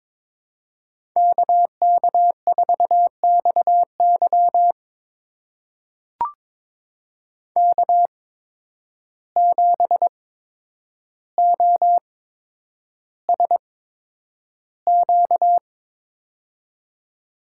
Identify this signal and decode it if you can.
Morse code, heard as a keyed tone, about 22 wpm, spaced as for 9 wpm, 700 Hz. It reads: KK4XY K7OSQ
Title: USA Call Signs - 5 Characters - 8x Character Spacing - 22wpm